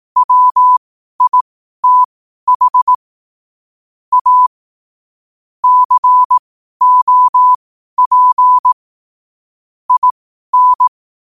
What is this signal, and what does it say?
Morse code, heard as a keyed tone, about 18 wpm, spaced as for 12 wpm, 1000 Hz. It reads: WITH A COP IN
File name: qotdfiles / 12wpm